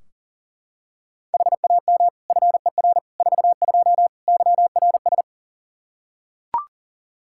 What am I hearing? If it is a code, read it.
Morse code, heard as a keyed tone, about 40 wpm, 690 Hz. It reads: HAM FER 42 YRS
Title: Practice QSO - Exchange 3 - More Info - 40wpm